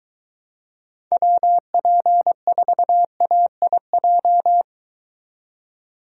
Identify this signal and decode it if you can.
Morse code, heard as a keyed tone, about 23 wpm, 700 Hz. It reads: WP4AIJ